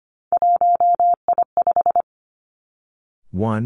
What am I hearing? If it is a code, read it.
Morse code, heard as a keyed tone, about 25 wpm, 700 Hz. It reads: 1I5
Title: Three Challenging Letter-Number Combinations 25wpm